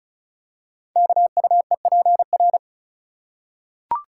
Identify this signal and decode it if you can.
Morse code, heard as a keyed tone, about 35 wpm, 690 Hz. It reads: KUEPR
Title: Code Groups - 5 Letters - 35wpm